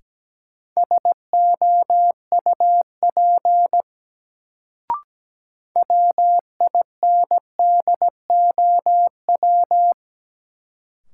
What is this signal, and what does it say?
Morse code, heard as a keyed tone, about 17 wpm, 700 Hz. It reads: SOUP WINDOW
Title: Sentences from Top 1600 Words - New Word Drill Alt - 17wpm